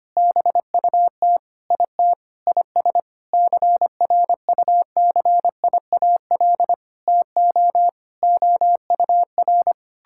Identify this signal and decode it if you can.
Morse code, heard as a keyed tone, about 25 wpm, 700 Hz. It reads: BUT IT IS CRUCIAL TO OUR
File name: practice / 25wpm / Sentences with 9-Word Phrases - 25wpm